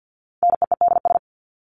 Morse code, heard as a keyed tone, about 50 wpm, 700 Hz. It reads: NEEDS
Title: ES:S Suffix - Top 100 Words with Base Words 50wpm